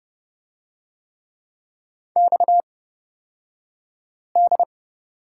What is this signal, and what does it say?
Morse code, heard as a keyed tone, about 31 wpm, spaced as for 5 wpm, 700 Hz. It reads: XD
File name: Lesson 37d - Letter x - Two Characters 30wpm